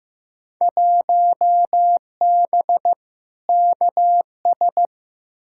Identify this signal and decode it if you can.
Morse code, heard as a keyed tone, about 16 wpm, 700 Hz. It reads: 1B KS
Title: Field Day Exchanges Encore - 15wpm